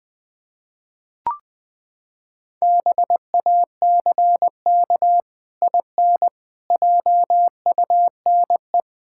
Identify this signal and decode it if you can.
Morse code, heard as a keyed tone, about 20 wpm, 700 Hz. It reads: BACK IN JUNE
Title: Pangram Challenge v2 - 20wpm